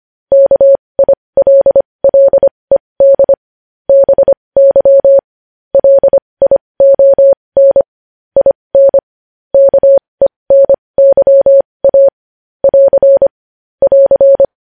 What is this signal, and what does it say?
Morse code, heard as a keyed tone, about 25 wpm, spaced as for 20 wpm, 550 Hz. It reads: KILLED BY LION IN KENYA <AR> <AR>
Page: news-headlines-in-morse-code-at-20-wpm